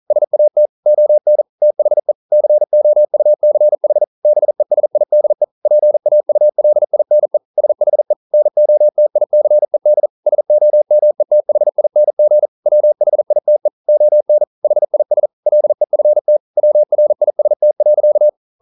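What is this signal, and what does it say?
Morse code, heard as a keyed tone, about 42 wpm, 600 Hz. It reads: SAT ON THE COUCH BESIDE PAULINE SHE NOTICED SOMETHING WHITE ON HIS LEFT WRIST.
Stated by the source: Morse code practice at 41 wpm, from CW med Gnister.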